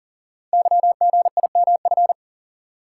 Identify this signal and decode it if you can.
Morse code, heard as a keyed tone, about 39 wpm, 700 Hz. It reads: YGIMF
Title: Code Groups - 5 Letter:Numbers - 40wpm